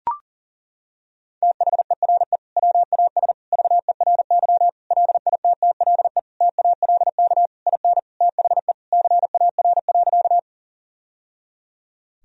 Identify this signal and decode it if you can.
Morse code, heard as a keyed tone, about 40 wpm, 700 Hz. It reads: THERE WAS VERY LITTLE TALK IN THE CAR.
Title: Sentences from Top 1000 Words - Review Part 1 - 40wpm